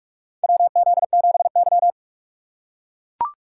Morse code, heard as a keyed tone, about 43 wpm, 700 Hz. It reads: WZ7Y
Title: USA Call Signs - 4x Character Spacing - 45wpm